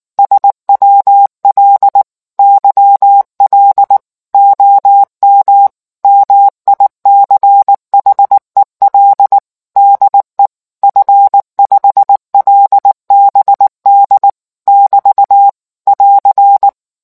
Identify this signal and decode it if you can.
Morse code, heard as a keyed tone, about 19 wpm, 800 Hz. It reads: SWL YL OM MICHEL DE F5LBD <BT> <AR>